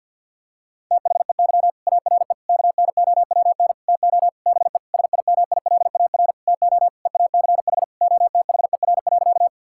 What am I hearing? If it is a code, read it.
Morse code, heard as a keyed tone, about 50 wpm, 700 Hz. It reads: THEY ARE KNOWN TO BE SIMILAR TO EACH OTHER.
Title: Sentences from Top 1000 Words - Review Part 5 - 50wpm